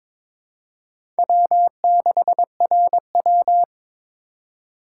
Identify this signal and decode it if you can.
Morse code, heard as a keyed tone, about 22 wpm, 700 Hz. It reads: W6RW
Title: USA Call Signs - 4 Characters - 4x Character Spacing - 22wpm